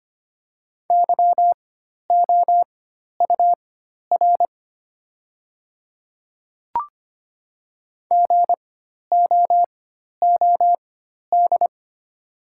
Morse code, heard as a keyed tone, about 25 wpm, spaced as for 12 wpm, 690 Hz. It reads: YOUR GOOD
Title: 4x Character Spacing - Top 200 Words - 25wpm